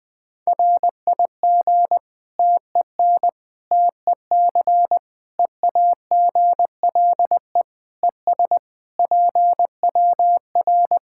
Morse code, heard as a keyed tone, about 21 wpm, 700 Hz. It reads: RIG TEN TEC EAGLE ES PWR
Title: K1USN SST QSO Practice 20wpm